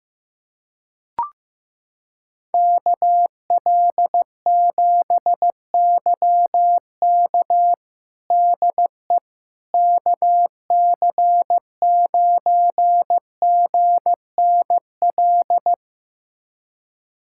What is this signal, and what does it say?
Morse code, heard as a keyed tone, about 15 wpm, 700 Hz. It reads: KL7YK DE KC9GNL
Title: USA Callsign DE Callsign 15wpm